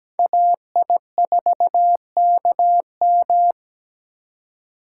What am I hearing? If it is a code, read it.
Morse code, heard as a keyed tone, about 17 wpm, 700 Hz. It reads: AI4KM
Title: USA Call Signs - 2x Character Spacing - 17wpm